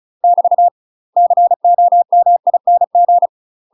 Morse code, heard as a keyed tone, about 33 wpm, 700 Hz. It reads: <BT> COMING
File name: UK MASTER QSOs TRAINING / 35WPM / UK MASTER QSOs 0013